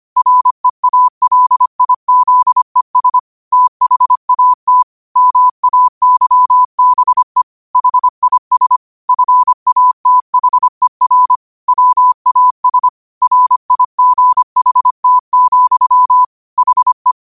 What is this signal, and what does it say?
Morse code, heard as a keyed tone, about 25 wpm, 1000 Hz. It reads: REALIZES THAT MAYBE HIS FATHER WAS RIGHT, HE